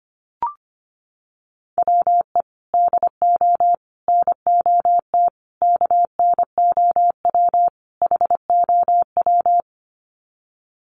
Morse code, heard as a keyed tone, about 24 wpm, 700 Hz. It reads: WE DO NOT KNOW HOW
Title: Sets of 5 Words Encore 25wpm